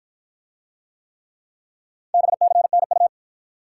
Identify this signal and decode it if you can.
Morse code, heard as a keyed tone, about 49 wpm, 700 Hz. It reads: BCNU